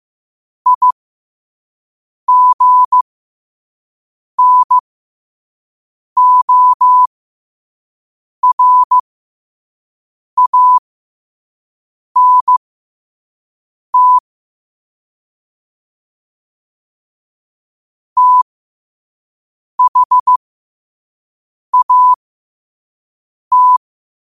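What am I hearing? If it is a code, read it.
Morse code, heard as a keyed tone, about 15 wpm, spaced as for 5 wpm, 1000 Hz. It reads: IGNORANT THAT
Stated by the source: Morse code quote-of-the-day practice at 5 wpm.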